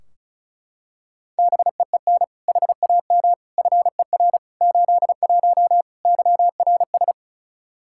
Morse code, heard as a keyed tone, about 36 wpm, 700 Hz. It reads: BEEN HAM FER 81 YRS
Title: Practice QSO - Exchange 3 - More Info - 35wpm